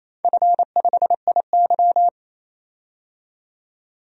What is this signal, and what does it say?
Morse code, heard as a keyed tone, about 28 wpm, 700 Hz. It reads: F5IY
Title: DXCC-100 Call Signs - 4 Characters - 28wpm